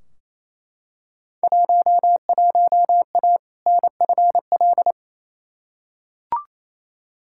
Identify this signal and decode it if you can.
Morse code, heard as a keyed tone, about 28 wpm, 700 Hz. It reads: 11A NFL
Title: Field Day Exchanges Encore - 28wpm